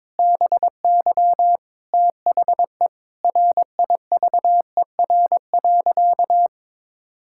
Morse code, heard as a keyed tone, about 22 wpm, 700 Hz. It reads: BY THE RIVER.
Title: Sentences from Top 1600 Words 22wpm